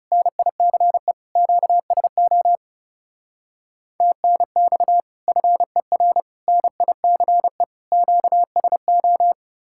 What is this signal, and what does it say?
Morse code, heard as a keyed tone, about 37 wpm, 700 Hz. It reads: NICE QSO TNX FER NICE QSO
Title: Speed-Racing Common QSO Phrases - 30wpm